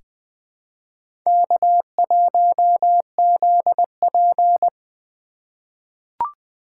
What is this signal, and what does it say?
Morse code, heard as a keyed tone, about 20 wpm, 700 Hz. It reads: K1ZP
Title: USA Call Signs - 4 Characters - 4x Character Spacing - 20wpm